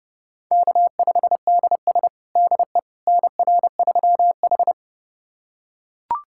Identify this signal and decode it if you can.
Morse code, heard as a keyed tone, about 30 wpm, 700 Hz. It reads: K5DS DE NR3H
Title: USA Callsign DE Callsign - 4 Characters - 30wpm